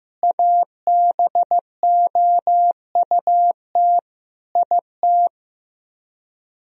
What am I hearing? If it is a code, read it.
Morse code, heard as a keyed tone, about 15 wpm, 700 Hz. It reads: ABOUT IT
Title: Speed-Racing - Sets of 5 Words 15wpm